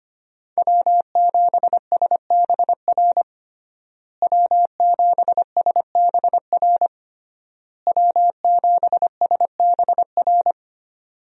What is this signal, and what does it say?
Morse code, heard as a keyed tone, about 25 wpm, 700 Hz. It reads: W7SBR W7SBR W7SBR
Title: USA Call Signs - 5 Characters - Repeated 3x - 25wpm